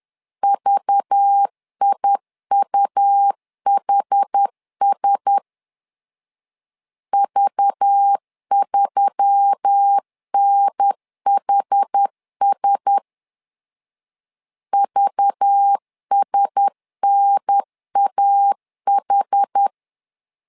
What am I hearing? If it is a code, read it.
Morse code, heard as a keyed tone, about 10 wpm, 800 Hz. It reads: VIUHS V3NHS VSNAH
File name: LEZIONI / 5 LBVC / 5 LBV 10wpm 2248